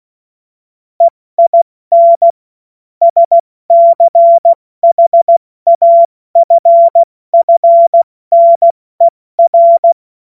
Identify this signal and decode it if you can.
Morse code, heard as a keyed tone, about 16 wpm, spaced as for 14 wpm, 680 Hz. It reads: EIN SCHAFFNER
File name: Morsekurs / Witze / Witz 05 Dezember 2023 70 F80